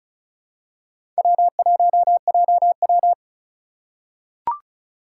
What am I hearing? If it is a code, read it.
Morse code, heard as a keyed tone, about 36 wpm, 700 Hz. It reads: W1JW
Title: USA Call Signs - 4 Characters - 2x Character Spacing - 35wpm